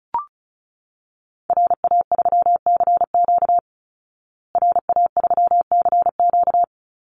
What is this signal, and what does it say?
Morse code, heard as a keyed tone, about 34 wpm, 700 Hz. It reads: RA3CQ RA3CQ
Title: Call Signs Repeated 3x - 35wpm